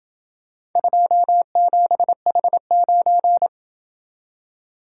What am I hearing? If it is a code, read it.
Morse code, heard as a keyed tone, about 27 wpm, 700 Hz. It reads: 27H9